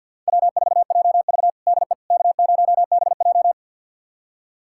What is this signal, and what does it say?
Morse code, heard as a keyed tone, about 48 wpm, 700 Hz. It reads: W4JV DE K0BJ
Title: USA Callsign DE Callsign - 4 Characters - 50wpm